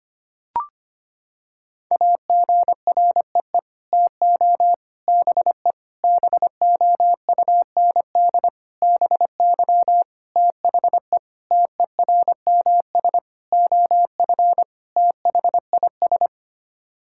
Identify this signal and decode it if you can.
Morse code, heard as a keyed tone, about 25 wpm, 700 Hz. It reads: AGREE TO BE BOUND BY THE TERMS OF THIS
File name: Sets of 9 Words - 25wpm